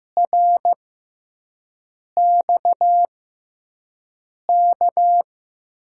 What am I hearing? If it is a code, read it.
Morse code, heard as a keyed tone, about 15 wpm, spaced as for 5 wpm, 700 Hz. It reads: RXK
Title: USA Call Signs - 6x Character Spacing - 15wpm